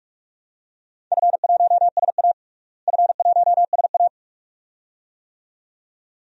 Morse code, heard as a keyed tone, about 46 wpm, 700 Hz. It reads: F1SA F1SA
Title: DXCC-100 Call Signs - 4 Characters - Repeated 2x - 45wpm